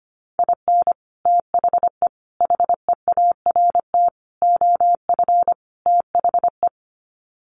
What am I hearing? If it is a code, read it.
Morse code, heard as a keyed tone, about 25 wpm, 700 Hz. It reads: IN THE HEART OF THE
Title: Sets of 5 Words 25wpm